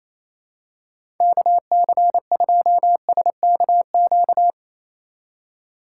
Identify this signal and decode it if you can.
Morse code, heard as a keyed tone, about 28 wpm, 700 Hz. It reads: KC2SKQ
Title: USA Call Signs - 6 Characters - 4x Character Spacing - 28wpm